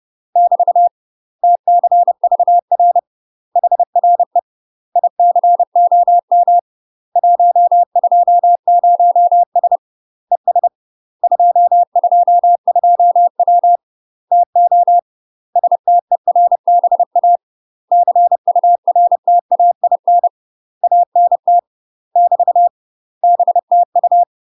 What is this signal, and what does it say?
Morse code, heard as a keyed tone, about 30 wpm, 700 Hz. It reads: <BT> TCVR HRE ICOM 120S ES 222W TO STERBA CURTAIN ANT <BT> BTU
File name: UK MASTER QSOs TRAINING / 30WPM / UK MASTER QSOs 0000